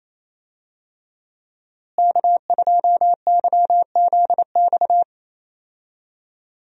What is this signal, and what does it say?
Morse code, heard as a keyed tone, about 28 wpm, 700 Hz. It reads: K2YZX
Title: USA Call Signs - 5 Characters - Repeated 3x - 28wpm